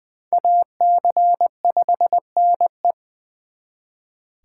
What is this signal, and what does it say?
Morse code, heard as a keyed tone, about 24 wpm, 700 Hz. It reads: AC5NE
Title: Speed Racing - USA 2x2 Call Signs 20wpm